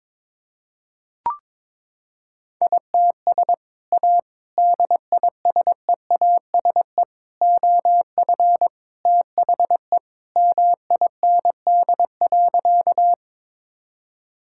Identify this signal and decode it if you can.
Morse code, heard as a keyed tone, about 22 wpm, 700 Hz. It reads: ITS A DISEASE OF THE MIND.
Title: Sentences from Top 1000 Words - Review Part 6 - 22wpm